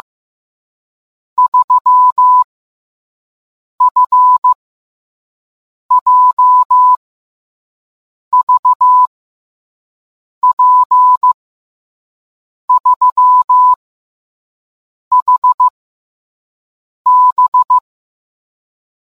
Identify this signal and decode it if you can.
Morse code, heard as a keyed tone, about 15 wpm, 1000 Hz. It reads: 3 F J V P 3 H B